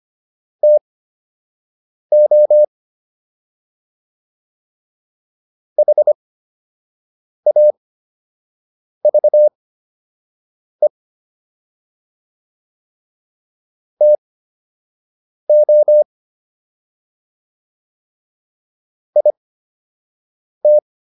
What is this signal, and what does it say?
Morse code, heard as a keyed tone, about 25 wpm, spaced as for 6 wpm, 600 Hz. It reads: TO HAVE TO IT